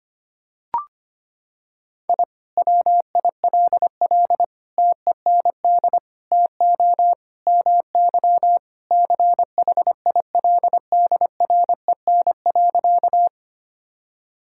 Morse code, heard as a keyed tone, about 25 wpm, 700 Hz. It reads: I WILL TEND TO MY CHILDREN.
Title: Sentences from Top 1000 Words - Review Part 5 - 25wpm